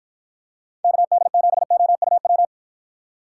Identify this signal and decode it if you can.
Morse code, heard as a keyed tone, about 52 wpm, 700 Hz. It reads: KD7OUW